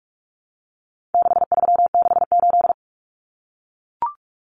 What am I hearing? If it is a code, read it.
Morse code, heard as a keyed tone, about 44 wpm, 700 Hz. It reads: 6368
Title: Common Cut Numbers - 4-Digit 45wpm